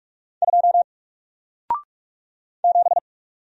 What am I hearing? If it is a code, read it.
Morse code, heard as a keyed tone, about 48 wpm, 700 Hz. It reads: 2 7
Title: Single Character (Letter:Number:Punctuation:Prosign) Alt 45wpm